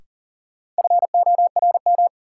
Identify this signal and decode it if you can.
Morse code, heard as a keyed tone, about 40 wpm, 700 Hz. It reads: FORM
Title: Sentences from Top 1000 Words - Review Part 1 - New Word Drill - 40wpm